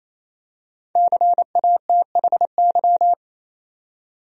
Morse code, heard as a keyed tone, about 28 wpm, 700 Hz. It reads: CATHY
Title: USA Common Names - Max 5 Letters - 28wpm